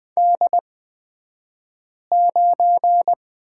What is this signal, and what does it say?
Morse code, heard as a keyed tone, about 20 wpm, spaced as for 5 wpm, 700 Hz. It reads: D9
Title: 242 Lesson 39d - Letter z - Two Characters 20wpm